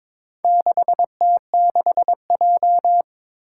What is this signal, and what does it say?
Morse code, heard as a keyed tone, about 22 wpm, 700 Hz. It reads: 6T6J
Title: Four Challenging Letter-Number Combinations 22wpm